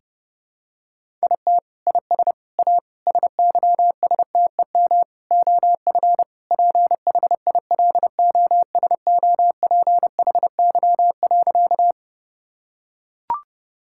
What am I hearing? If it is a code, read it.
Morse code, heard as a keyed tone, about 32 wpm, 700 Hz. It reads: IT IS A SYSTEM OF PHILOSOPHY.
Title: Sentences from Top 1000 Words - Review Part 6 - 30wpm